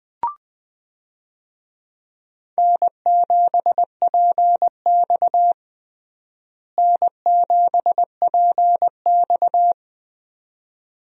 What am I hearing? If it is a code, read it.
Morse code, heard as a keyed tone, about 21 wpm, 700 Hz. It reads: N7PX N7PX
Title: USA Call Signs - 4 Characters - Repeated 2x - 20wpm